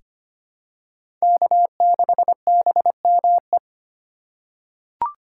Morse code, heard as a keyed tone, about 25 wpm, 700 Hz. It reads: K6BME
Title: USA Call Signs - 5 Characters - 4x Character Spacing - 25wpm